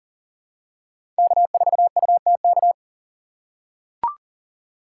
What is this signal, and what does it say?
Morse code, heard as a keyed tone, about 40 wpm, 700 Hz. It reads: K4UTK
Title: USA Call Signs - 5 Characters - 2x Character Spacing - 40wpm